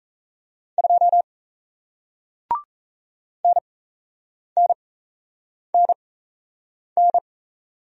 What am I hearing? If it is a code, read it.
Morse code, heard as a keyed tone, about 38 wpm, spaced as for 8 wpm, 700 Hz. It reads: 2 NNNN